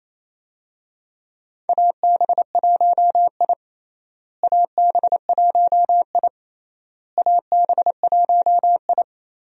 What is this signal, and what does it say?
Morse code, heard as a keyed tone, about 28 wpm, 700 Hz. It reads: AB1I AB1I AB1I
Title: USA Call Signs - 4 Characters - Repeated 3x - 28wpm